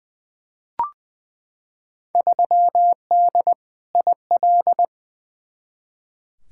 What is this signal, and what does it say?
Morse code, heard as a keyed tone, about 20 wpm, 700 Hz. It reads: 3D IL
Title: Field Day Exchanges Encore - 20wpm